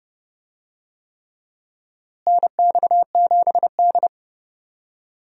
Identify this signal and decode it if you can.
Morse code, heard as a keyed tone, about 30 wpm, 700 Hz. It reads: NX7D
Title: USA Call Signs - 4 Characters - Repeated 3x - 30wpm